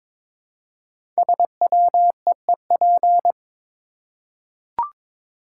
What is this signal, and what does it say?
Morse code, heard as a keyed tone, about 22 wpm, 690 Hz. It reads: SWEEP
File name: 5 Letter Words - First 3 Letters Given - 22wpm